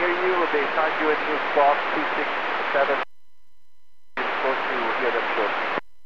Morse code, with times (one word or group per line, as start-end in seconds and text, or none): none